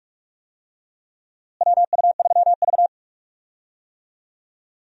Morse code, heard as a keyed tone, about 44 wpm, 700 Hz. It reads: WU3V
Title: USA Call Signs - 4 Characters - Repeated 3x - 45wpm